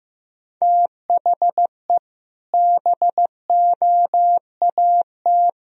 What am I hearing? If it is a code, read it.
Morse code, heard as a keyed tone, about 15 wpm, 700 Hz. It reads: THE BOAT
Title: Sentences from Top 1000 Words - Review Part 1 - 15wpm